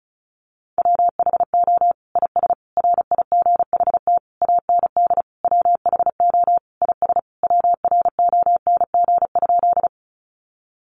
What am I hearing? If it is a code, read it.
Morse code, heard as a keyed tone, about 34 wpm, 700 Hz. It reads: WHO IS RIGHT AND WHO IS WRONG?
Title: Sentences from Top 900 Words 35wpm